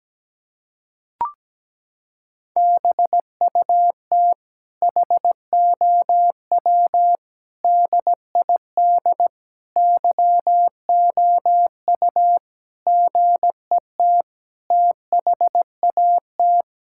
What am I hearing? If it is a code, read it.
Morse code, heard as a keyed tone, about 17 wpm, 700 Hz. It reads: BUT HOW DID YOU GET THAT
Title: Sentences from Top 1000 Words - Review Part 6 - 17wpm